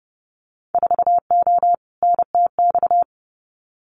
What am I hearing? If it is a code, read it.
Morse code, heard as a keyed tone, about 30 wpm, 700 Hz. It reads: 4O NTX
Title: Winter Field Day Exchanges 30wpm